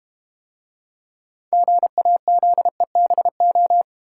Morse code, heard as a keyed tone, about 32 wpm, 700 Hz. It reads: GAZEBO